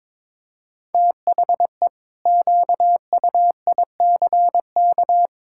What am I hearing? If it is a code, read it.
Morse code, heard as a keyed tone, about 22 wpm, 700 Hz. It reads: THE QUICK
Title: Pangram Challenge v2 - 22wpm